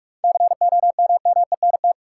Morse code, heard as a keyed tone, about 44 wpm, 700 Hz. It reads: COMMENT
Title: Seven Letter Words - 45wpm